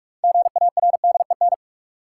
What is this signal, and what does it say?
Morse code, heard as a keyed tone, about 46 wpm, 700 Hz. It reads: GARDEN
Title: Sentences from Top 700 Words - New Word Drill - 45wpm